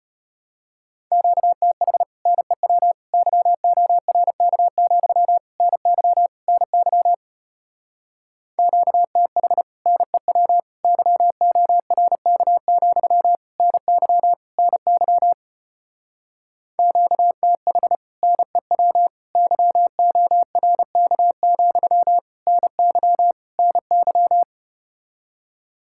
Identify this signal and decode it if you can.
Morse code, heard as a keyed tone, about 33 wpm, 700 Hz. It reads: QTH NEW YORK, NY NY QTH NEW YORK, NY NY QTH NEW YORK, NY NY